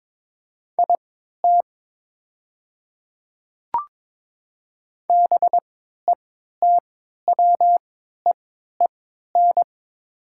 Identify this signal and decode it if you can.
Morse code, heard as a keyed tone, about 22 wpm, spaced as for 12 wpm, 700 Hz. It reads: IT BETWEEN
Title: 3x Character Spacing - Top 500 Words - 22wpm